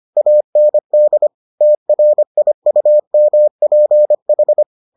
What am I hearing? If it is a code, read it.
Morse code, heard as a keyed tone, about 25 wpm, 600 Hz. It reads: AND TRIUMPH